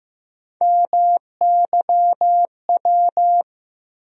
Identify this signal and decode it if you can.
Morse code, heard as a keyed tone, about 15 wpm, 700 Hz. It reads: MYW
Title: USA Call Sign Postfixes - 3 Letters 15wpm